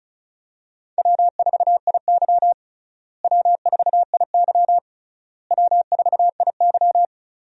Morse code, heard as a keyed tone, about 34 wpm, 700 Hz. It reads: W4IY W4IY W4IY
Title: USA Call Signs - 4 Characters - Repeated 3x - 35wpm